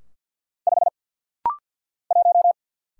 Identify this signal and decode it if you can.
Morse code, heard as a keyed tone, about 49 wpm, 700 Hz. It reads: 5 1
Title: Single Letter-Number - Rapid-Fire 50wpm